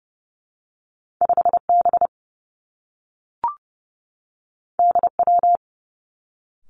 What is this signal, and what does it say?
Morse code, heard as a keyed tone, about 30 wpm, 700 Hz. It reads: 5 B D W
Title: Two Challenging Letter-Number Combinations 30wpm